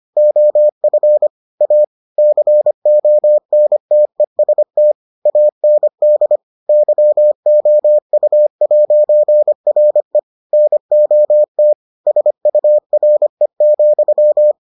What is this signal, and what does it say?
Morse code, heard as a keyed tone, about 25 wpm, 600 Hz. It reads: OF A CONTEST AND YOU'RE NOT SURE,